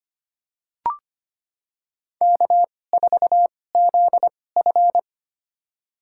Morse code, heard as a keyed tone, about 25 wpm, spaced as for 18 wpm, 700 Hz. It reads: K4ZF
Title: USA Call Signs - 4 Characters - 2x Character Spacing - 25wpm